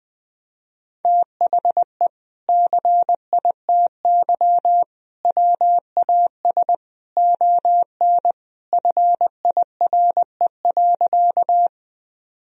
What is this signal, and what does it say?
Morse code, heard as a keyed tone, about 21 wpm, 700 Hz. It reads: THE CITY WAS ON FIRE.
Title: Sentences from Top 1000 Words - Review Part 1 - 20wpm